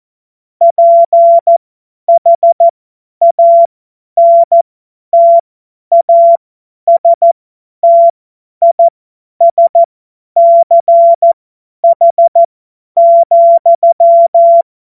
Morse code, heard as a keyed tone, about 14 wpm, spaced as for 10 wpm, 680 Hz. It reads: PHANTASTISCH,
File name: Morsekurs / Witze / Witz 21. März 2023 50 F70